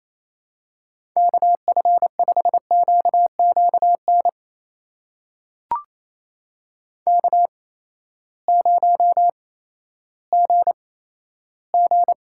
Morse code, heard as a keyed tone, about 28 wpm, spaced as for 14 wpm, 700 Hz. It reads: KF5QQN K0GG
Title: USA Call Signs - 8x Character Spacing - 28wpm